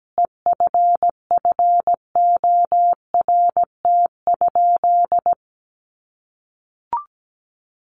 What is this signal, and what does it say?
Morse code, heard as a keyed tone, about 17 wpm, 700 Hz. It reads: EFFORT?
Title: Sentences from Top 1200 Words 17wpm